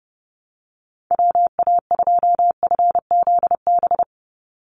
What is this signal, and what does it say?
Morse code, heard as a keyed tone, about 30 wpm, 700 Hz. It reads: WA2FZB
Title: Call Signs - 30wpm